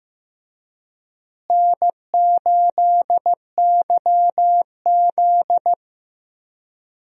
Morse code, heard as a keyed tone, about 15 wpm, 700 Hz. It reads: N8YZ
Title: USA Call Signs - 4 Characters - Repeated 3x - 15wpm